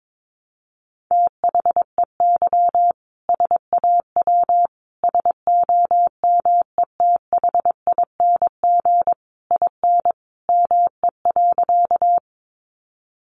Morse code, heard as a keyed tone, about 22 wpm, 700 Hz. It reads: THEY SAW SOMETHING IN ME.
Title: Sentences from Top 600 Words 22wpm